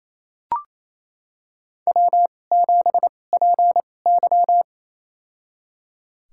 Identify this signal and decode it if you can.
Morse code, heard as a keyed tone, about 28 wpm, spaced as for 20 wpm, 700 Hz. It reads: W7PY
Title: USA Call Signs - 4 Characters - 2x Character Spacing - 28wpm